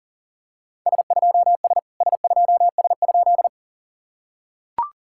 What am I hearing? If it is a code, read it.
Morse code, heard as a keyed tone, about 40 wpm, 690 Hz. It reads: S2S S2S?
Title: SOTA QSO Practice 40wpm